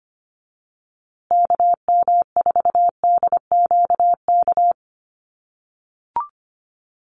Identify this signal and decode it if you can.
Morse code, heard as a keyed tone, about 25 wpm, 700 Hz. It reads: KM4DQK
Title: USA Call Signs Repeated 2x 25wpm